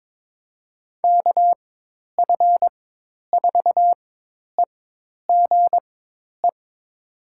Morse code, heard as a keyed tone, about 22 wpm, spaced as for 10 wpm, 700 Hz. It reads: KF4EGE
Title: USA Call Signs - 6 Characters - 4x Character Spacing - 22wpm